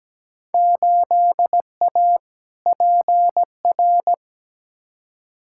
Morse code, heard as a keyed tone, about 17 wpm, 700 Hz. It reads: 8A PR
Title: Field Day Exchanges Encore - 17wpm